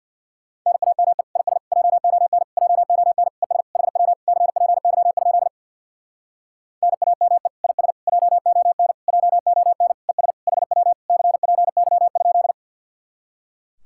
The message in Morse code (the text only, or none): NAME IS JON JON ES HW CPY? NAME IS JON JON ES HW CPY?